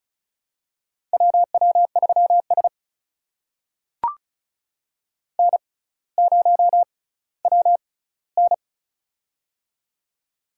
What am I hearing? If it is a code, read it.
Morse code, heard as a keyed tone, about 36 wpm, spaced as for 18 wpm, 700 Hz. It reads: WW3S N0WN
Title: USA Call Signs - 4 Characters - 6x Character Spacing - 35wpm